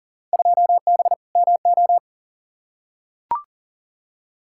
Morse code, heard as a keyed tone, about 39 wpm, 700 Hz. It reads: 2B MO
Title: Field Day Exchanges - 40wpm